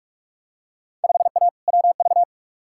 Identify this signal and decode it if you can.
Morse code, heard as a keyed tone, about 48 wpm, 700 Hz. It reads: 5A WV
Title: Field Day Exchanges - 45wpm